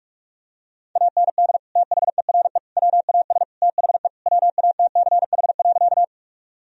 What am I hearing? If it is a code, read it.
Morse code, heard as a keyed tone, about 41 wpm, 700 Hz. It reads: AND THERE WAS THE WATCH.
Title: Sentences from Top 1000 Words - Review Part 1 - 45wpm